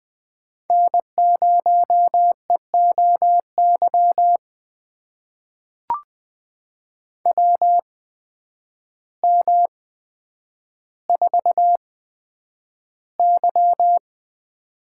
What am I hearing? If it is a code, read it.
Morse code, heard as a keyed tone, about 20 wpm, spaced as for 9 wpm, 700 Hz. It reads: N0EOY WM4Y
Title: USA Call Signs - 8x Character Spacing - 20wpm